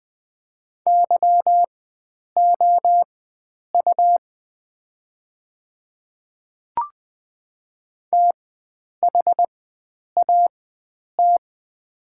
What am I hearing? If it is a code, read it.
Morse code, heard as a keyed tone, about 20 wpm, spaced as for 9 wpm, 700 Hz. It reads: YOU THAT
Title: 4x Character Spacing - Top 300 Words - 20wpm